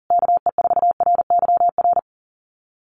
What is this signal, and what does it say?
Morse code, heard as a keyed tone, about 40 wpm, 700 Hz. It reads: KE4RYR
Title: USA Call Signs Repeated 3x 40wpm